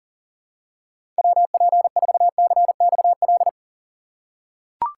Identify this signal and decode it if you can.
Morse code, heard as a keyed tone, about 40 wpm, 700 Hz. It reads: WP4CXL
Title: USA Call Signs - 6 Characters - 2x Character Spacing - 40wpm